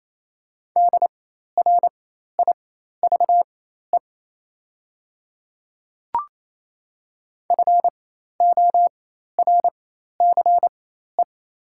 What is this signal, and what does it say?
Morse code, heard as a keyed tone, about 28 wpm, spaced as for 13 wpm, 700 Hz. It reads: DRIVE FORCE
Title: 4x Character Spacing - Top 500 Words - 28wpm